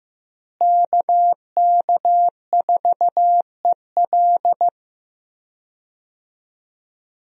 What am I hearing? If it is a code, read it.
Morse code, heard as a keyed tone, about 15 wpm, 700 Hz. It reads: KK4EL
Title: USA Call Signs - 5 Characters - Repeated 3x - 15wpm